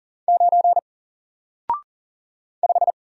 Morse code, heard as a keyed tone, about 40 wpm, 700 Hz. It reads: 9 5
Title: Single Letter-Number Alt 40wpm